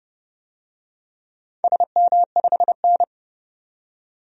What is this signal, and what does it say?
Morse code, heard as a keyed tone, about 31 wpm, 700 Hz. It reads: SM5N
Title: DXCC-100 Call Signs - 4 Characters - 30wpm